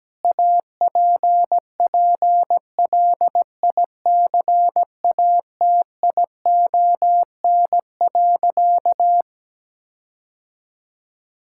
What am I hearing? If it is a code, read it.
Morse code, heard as a keyed tone, about 17 wpm, 700 Hz. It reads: APPLICATION.
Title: Sentences from Top 1600 Words 17wpm